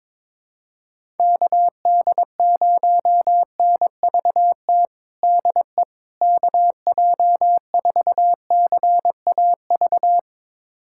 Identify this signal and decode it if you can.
Morse code, heard as a keyed tone, about 22 wpm, 700 Hz. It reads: KD0NVT DE KJ4CAV
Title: USA Callsign DE Callsign 22wpm